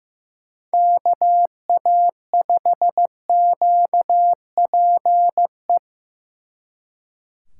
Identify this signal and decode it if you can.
Morse code, heard as a keyed tone, about 18 wpm, 700 Hz. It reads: KA5QPE
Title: Speed Racing - USA 2x3 Call Signs 15wpm